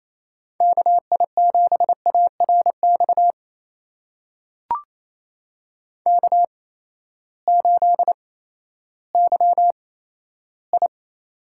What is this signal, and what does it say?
Morse code, heard as a keyed tone, about 28 wpm, spaced as for 14 wpm, 700 Hz. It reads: KI7ARX K8YI
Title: USA Call Signs - 8x Character Spacing - 28wpm